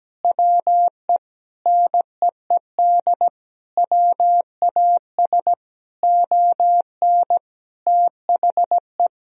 Morse code, heard as a keyed tone, about 17 wpm, 700 Hz. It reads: WE NEED WAS ON THE
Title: Sentences from Top 1000 Words - Review Part 1 - 17wpm